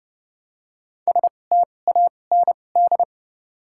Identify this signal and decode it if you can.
Morse code, heard as a keyed tone, about 30 wpm, spaced as for 22 wpm, 700 Hz. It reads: STAND
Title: 2x Character Spacing - Top 500 Words - 30wpm